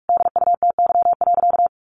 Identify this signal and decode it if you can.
Morse code, heard as a keyed tone, about 44 wpm, 700 Hz. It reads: DUTY.
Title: Sentences from Top 1500 Words 45wpm